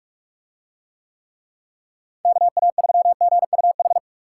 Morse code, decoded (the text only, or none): KA3GUH